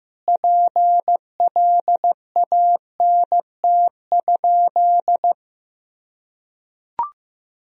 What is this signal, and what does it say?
Morse code, heard as a keyed tone, about 15 wpm, 700 Hz. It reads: PLANT?
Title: Sentences from Top 1000 Words - Review Part 1 - 15wpm